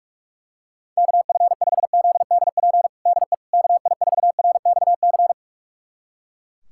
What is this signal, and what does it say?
Morse code, heard as a keyed tone, about 42 wpm, 690 Hz. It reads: KF5ZDP DE KI4RXC
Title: USA Callsign DE Callsign 45wpm